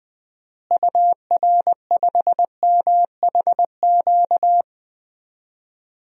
Morse code, heard as a keyed tone, about 20 wpm, 700 Hz. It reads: UR5MHQ
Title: DXCC-100 Call Signs - 6 Characters - Repeated 3x - 20wpm